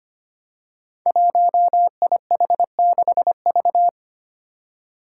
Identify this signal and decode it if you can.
Morse code, heard as a keyed tone, about 25 wpm, 700 Hz. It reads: 1IH6V
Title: Five Challenging Letter-Number Combinations 25wpm